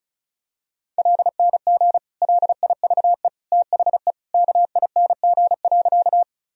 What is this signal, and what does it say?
Morse code, heard as a keyed tone, about 36 wpm, 700 Hz. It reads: LNG LIVE THE KING.
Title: Sentences from Top 400 Words and Abbreviations 35wpm